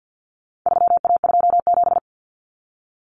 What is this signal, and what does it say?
Morse code, heard as a keyed tone, about 50 wpm, 700 Hz. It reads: 3A27
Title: Common Cut Numbers - 4-Digit 50wpm